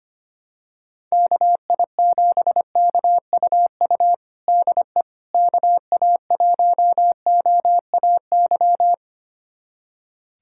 Morse code, heard as a keyed tone, about 25 wpm, 700 Hz. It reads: KI7KUU DE KA1OAY
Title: USA Callsign DE Callsign 25wpm